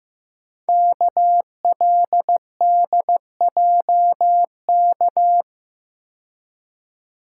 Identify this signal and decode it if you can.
Morse code, heard as a keyed tone, about 15 wpm, 700 Hz. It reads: KLDJK
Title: Code Groups - 5 Letters - 15wpm